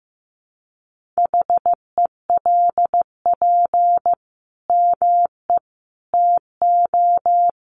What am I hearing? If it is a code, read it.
Morse code, heard as a keyed tone, about 15 wpm, 700 Hz. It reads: HELP ME TO
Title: Sentences from Top 1100 Words 15wpm